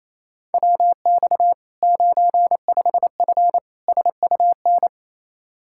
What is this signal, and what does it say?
Morse code, heard as a keyed tone, about 27 wpm, 700 Hz. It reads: WX 95F SUN
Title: Practice QSO - Exchange 2 - Info - 28wpm